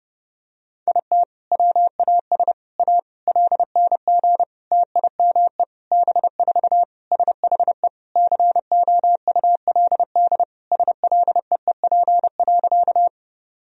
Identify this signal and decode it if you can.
Morse code, heard as a keyed tone, about 30 wpm, 700 Hz. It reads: IT WAS A LNG TIME B4 SHE COULD SLEEP.
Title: Sentences from Top 500 Words and Abbreviations 30wpm